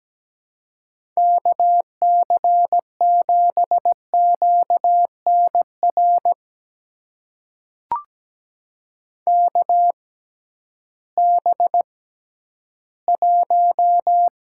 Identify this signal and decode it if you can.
Morse code, heard as a keyed tone, about 17 wpm, spaced as for 11 wpm, 700 Hz. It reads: KC7QNR KB1
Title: USA Call Signs - 6 Characters - 6x Character Spacing - 17wpm